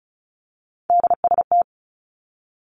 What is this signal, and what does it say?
Morse code, heard as a keyed tone, about 36 wpm, 700 Hz. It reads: DST
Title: Three Challenging Letter-Number Combinations 35wpm